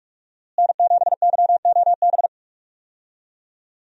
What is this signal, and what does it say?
Morse code, heard as a keyed tone, about 42 wpm, 700 Hz. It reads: N7YOB
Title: USA Call Signs - 5 Characters - Repeated 2x - 45wpm